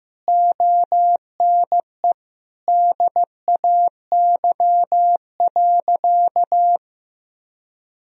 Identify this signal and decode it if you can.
Morse code, heard as a keyed tone, about 15 wpm, 700 Hz. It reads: ONE DAY.
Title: Sentences from Top 1000 Words - Review Part 4 - 15wpm